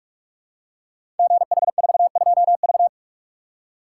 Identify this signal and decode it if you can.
Morse code, heard as a keyed tone, about 44 wpm, 700 Hz. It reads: GH42V
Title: Code Groups - 5 Letter:Numbers - 45wpm